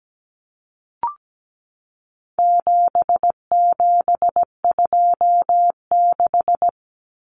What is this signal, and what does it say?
Morse code, heard as a keyed tone, about 17 wpm, 700 Hz. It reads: 7726
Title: Numbers - 4-Digit 17wpm